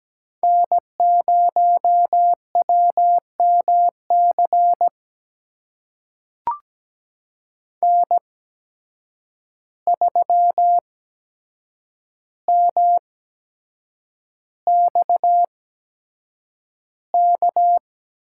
Morse code, heard as a keyed tone, about 17 wpm, spaced as for 7 wpm, 700 Hz. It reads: N0WMC N3MXK
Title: USA Call Signs - 5 Characters - 8x Character Spacing - 17wpm